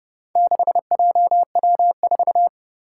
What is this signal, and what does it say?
Morse code, heard as a keyed tone, about 30 wpm, 700 Hz. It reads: 6JW4
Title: Four Challenging Letter-Number Combinations 30wpm